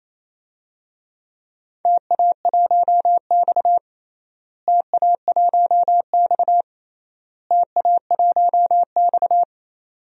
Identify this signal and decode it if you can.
Morse code, heard as a keyed tone, about 28 wpm, 700 Hz. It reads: TA1X TA1X TA1X
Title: DXCC-100 Call Signs - 4 Characters - Repeated 3x - 28wpm